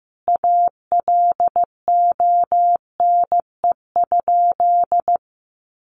Morse code, heard as a keyed tone, about 14 wpm, 700 Hz. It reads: ALONE?
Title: Sentences from Top 1000 Words 15wpm